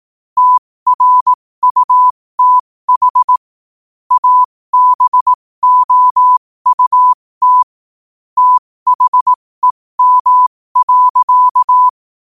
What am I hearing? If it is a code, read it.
Morse code, heard as a keyed tone, about 18 wpm, spaced as for 15 wpm, 1000 Hz. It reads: TRUTH ABOUT THEM.
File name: qotdfiles / 15wpm